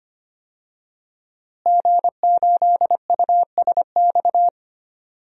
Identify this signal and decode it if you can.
Morse code, heard as a keyed tone, about 25 wpm, 700 Hz. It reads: G8USX
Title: DXCC-100 Call Signs - 5 Characters - Repeated 3x - 25wpm